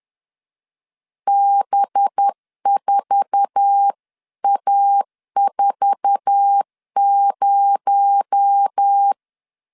Morse code, heard as a keyed tone, about 10 wpm, 800 Hz. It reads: B4A40
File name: LEZIONI / 8 YJ45 / 8 YJ4 10wpm 2633